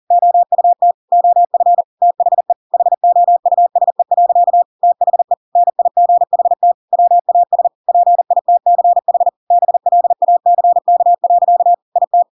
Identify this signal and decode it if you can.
Morse code, heard as a keyed tone, about 40 wpm, 700 Hz. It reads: OUT OF THE HOUSE. THE NIGHT WAS PITCH BLACK. IT